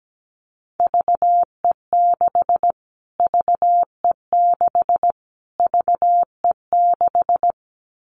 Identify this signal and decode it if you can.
Morse code, heard as a keyed tone, about 17 wpm, 700 Hz. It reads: VE6 VE6 VE6
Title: Callsign Prefixes for Top 25 DXCC - Repeated 3x - 17wpm